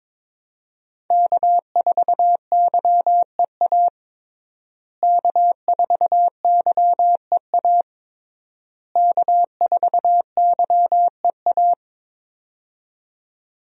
Morse code, heard as a keyed tone, about 22 wpm, 700 Hz. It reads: K4YEA K4YEA K4YEA
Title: USA Call Signs - 5 Characters - Repeated 3x - 22wpm